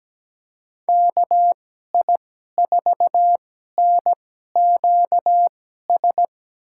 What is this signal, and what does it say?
Morse code, heard as a keyed tone, about 17 wpm, spaced as for 12 wpm, 700 Hz. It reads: KI4NQS
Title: USA Call Signs - 6 Characters - 2x Character Spacing - 17wpm